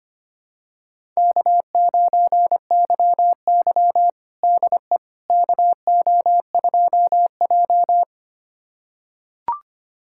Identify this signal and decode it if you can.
Morse code, heard as a keyed tone, about 25 wpm, 700 Hz. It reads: K9YY DE KO2J
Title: USA Callsign DE Callsign - 4 Characters - 25wpm